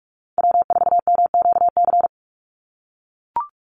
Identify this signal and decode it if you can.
Morse code, heard as a keyed tone, about 44 wpm, 700 Hz. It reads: W4MQC
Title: Call Signs - 45wpm